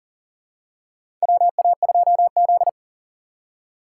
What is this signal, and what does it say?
Morse code, heard as a keyed tone, about 40 wpm, 700 Hz. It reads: WA2Z
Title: USA Call Signs - 4x Character Spacing - 40wpm